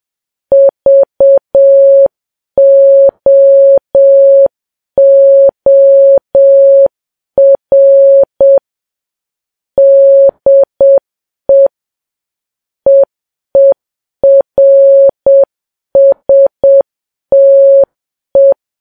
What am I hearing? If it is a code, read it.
Morse code, heard as a keyed tone, about 7 wpm, 550 Hz. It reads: VOOR DE EERSTE
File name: Files Web tls / Morse on9fv / morse7kps